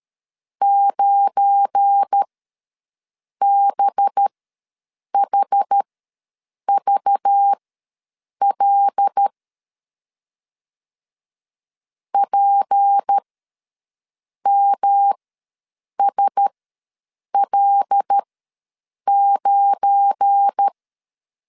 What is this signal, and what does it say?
Morse code, heard as a keyed tone, about 12 wpm, spaced as for 7 wpm, 800 Hz. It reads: 9BHVL PMSL9
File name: LEZIONI / 7 GF9Z / 7 GF9 ws 12wpm 3344